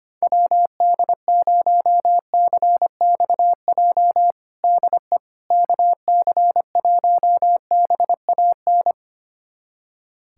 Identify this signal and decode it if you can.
Morse code, heard as a keyed tone, about 25 wpm, 700 Hz. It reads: WD0CXJ DE KC1BAN
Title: USA Callsign DE Callsign - 6 Characters - 25wpm